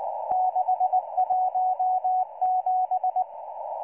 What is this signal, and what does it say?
Morse code, heard as a keyed tone, about 19 wpm, 740 Hz. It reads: 617